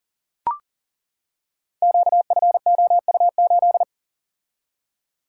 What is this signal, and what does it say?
Morse code, heard as a keyed tone, about 42 wpm, 700 Hz. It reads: QFOU8
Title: Code Groups - 5 Letter:Numbers - 40wpm